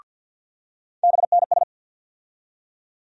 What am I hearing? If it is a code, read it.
Morse code, heard as a keyed tone, about 52 wpm, 700 Hz. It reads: BNS
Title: USA Call Sign Postfixes - 3 Letters 50wpm